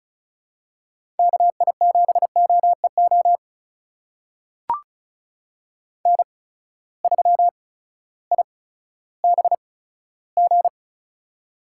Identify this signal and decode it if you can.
Morse code, heard as a keyed tone, about 36 wpm, spaced as for 16 wpm, 700 Hz. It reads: KI7OEO N3IBG
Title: USA Call Signs - 8x Character Spacing - 35wpm